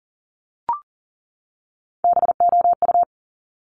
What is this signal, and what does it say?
Morse code, heard as a keyed tone, about 38 wpm, 690 Hz. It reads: BOU
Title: Top Three Letter Combinations in Words 40wpm